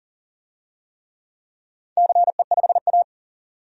Morse code, heard as a keyed tone, about 40 wpm, 700 Hz. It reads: CE5A
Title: DXCC-100 Call Signs - 4 Characters - 40wpm